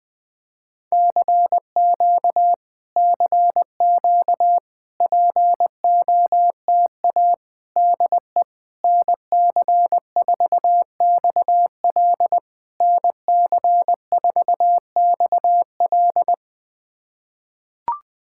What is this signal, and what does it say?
Morse code, heard as a keyed tone, about 20 wpm, 690 Hz. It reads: CQ CQ POTA DE NC4XL NC4XL
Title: POTA QSO Practice - 20wpm